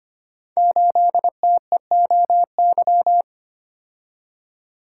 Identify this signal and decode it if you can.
Morse code, heard as a keyed tone, about 25 wpm, 700 Hz. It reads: 8TEOY
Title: Code Groups - 5 Letter:Numbers - 25wpm